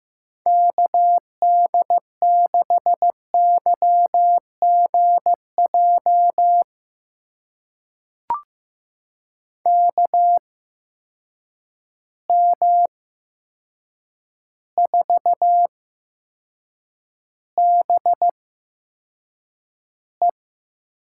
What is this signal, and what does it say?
Morse code, heard as a keyed tone, about 15 wpm, spaced as for 7 wpm, 700 Hz. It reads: KD6YGJ KM4BE
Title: USA Call Signs - 6 Characters - 8x Character Spacing - 15wpm